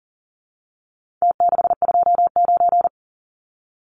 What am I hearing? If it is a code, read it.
Morse code, heard as a keyed tone, about 40 wpm, 700 Hz. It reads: T629
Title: Numbers - Contest Style - 4-Digit 40wpm